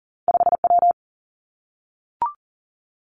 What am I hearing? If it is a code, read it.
Morse code, heard as a keyed tone, about 41 wpm, 700 Hz. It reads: 5 W
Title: Two Challenging Letter-Number Combinations 40wpm